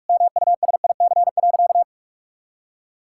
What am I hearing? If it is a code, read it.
Morse code, heard as a keyed tone, about 46 wpm, 700 Hz. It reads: MUSIC.
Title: Sentences from Top 1000 Words - Review Part 6 - 45wpm